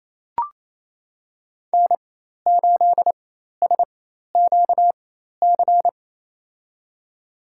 Morse code, heard as a keyed tone, about 28 wpm, spaced as for 13 wpm, 700 Hz. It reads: N8SQC
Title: USA Call Signs - 4x Character Spacing - 28wpm